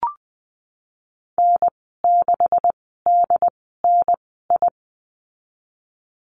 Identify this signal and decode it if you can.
Morse code, heard as a keyed tone, about 20 wpm, spaced as for 15 wpm, 700 Hz. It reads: N6DNI
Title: 167 Lesson 27g - Number 7 - Call Signs 20wpm